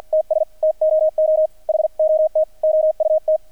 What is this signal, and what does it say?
Morse code, heard as a keyed tone, about 48 wpm, 630 Hz. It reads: TS TOO HOT OUT
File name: archived / CRSnet